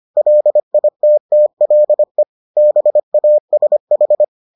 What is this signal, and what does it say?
Morse code, heard as a keyed tone, about 25 wpm, 600 Hz. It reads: LITTLE BASH